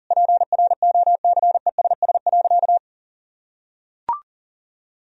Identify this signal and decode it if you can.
Morse code, heard as a keyed tone, about 40 wpm, 700 Hz. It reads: PROCESS.
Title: Sentences from Top 1000 Words - Review Part 4 - 40wpm